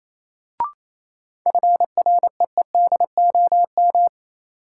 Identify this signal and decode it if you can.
Morse code, heard as a keyed tone, about 28 wpm, 700 Hz. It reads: FREEDOM
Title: Sentences from Top 800 Words - New Word Drill - 28wpm